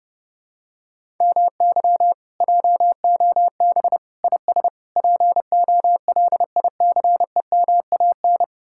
Morse code, heard as a keyed tone, about 32 wpm, 700 Hz. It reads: MY JOB IS POLICEMAN